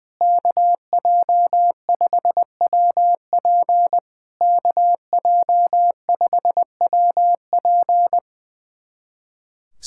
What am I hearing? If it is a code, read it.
Morse code, heard as a keyed tone, about 20 wpm, 700 Hz. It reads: KJ5WP KJ5WP
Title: SOTA QSO Practice 20wpm